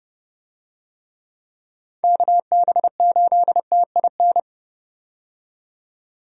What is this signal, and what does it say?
Morse code, heard as a keyed tone, about 30 wpm, 700 Hz. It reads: KB8TIN
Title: USA Call Signs - 6 Characters - 30wpm